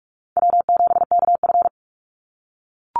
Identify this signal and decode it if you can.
Morse code, heard as a keyed tone, about 44 wpm, 700 Hz. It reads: W7KF
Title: Easy Call Signs - 45wpm